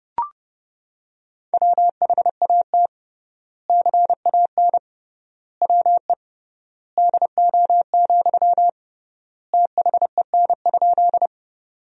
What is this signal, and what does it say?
Morse code, heard as a keyed tone, about 32 wpm, 700 Hz. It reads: WHAT CAN WE DO, THEN?
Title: 3x Word-Spacing - Sentences from Top 100 Words - 30wpm